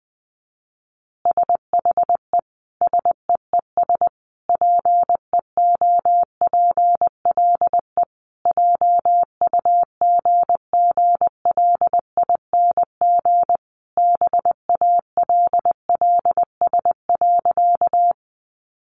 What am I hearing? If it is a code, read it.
Morse code, heard as a keyed tone, about 20 wpm, 700 Hz. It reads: SHE SEES PEOPLE JUGGLING BALLS.
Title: Easy Phrases 20wpm